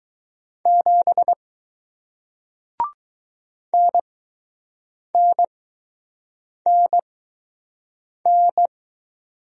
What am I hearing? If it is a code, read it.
Morse code, heard as a keyed tone, about 20 wpm, spaced as for 6 wpm, 700 Hz. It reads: 7 NNNN